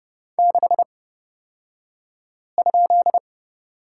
Morse code, heard as a keyed tone, about 30 wpm, spaced as for 5 wpm, 700 Hz. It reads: 6?
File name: Lesson 38d - Letter q - Two Characters 30wpm